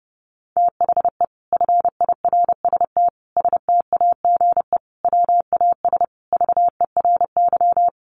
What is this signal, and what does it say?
Morse code, heard as a keyed tone, about 28 wpm, 700 Hz. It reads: THE FIRST STAGE WAS VERY
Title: Sentences from Top 1200 Words 30wpm